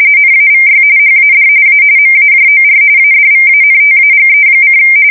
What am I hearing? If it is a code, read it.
RTTY, about 45.45 baud. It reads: NAVY HAD A MAJOR ADVANTAGE:
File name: fsk170 / usnavy4